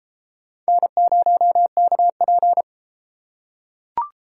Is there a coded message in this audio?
Yes: Morse code from 0.7 to 2.6 s, heard as a keyed tone, about 31 wpm, 700 Hz. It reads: N0KP